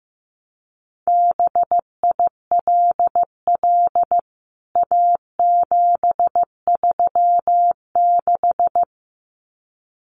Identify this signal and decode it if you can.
Morse code, heard as a keyed tone, about 15 wpm, 700 Hz. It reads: BILL A736
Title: CWT Exchange 15wpm